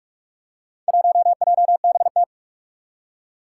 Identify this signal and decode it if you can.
Morse code, heard as a keyed tone, about 44 wpm, 700 Hz. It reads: 1JBT
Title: Four Challenging Letter-Number Combinations 45wpm